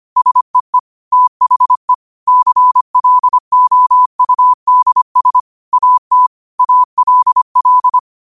Morse code, heard as a keyed tone, about 25 wpm, 1000 Hz. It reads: SEE THE CLOUDS AT ALL
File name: qotdfiles / 25wpm